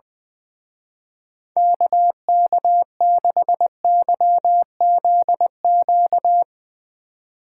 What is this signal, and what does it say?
Morse code, heard as a keyed tone, about 22 wpm, 700 Hz. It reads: KK6YZQ
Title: Speed Racing - USA 2x3 Call Signs 20wpm